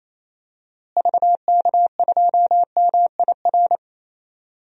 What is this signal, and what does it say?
Morse code, heard as a keyed tone, about 28 wpm, 700 Hz. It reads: VK2MIR
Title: DXCC-100 Call Signs - 6 Characters - Repeated 2x - 28wpm